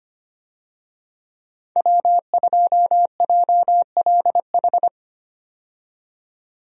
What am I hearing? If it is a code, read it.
Morse code, heard as a keyed tone, about 25 wpm, 700 Hz. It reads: W2JLH
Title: USA Call Signs - 5 Characters - 25wpm